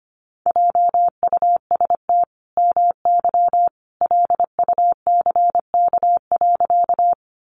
Morse code, heard as a keyed tone, about 25 wpm, 700 Hz. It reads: JUST MY LUCK.
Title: Sentences from Top 1500 Words 25wpm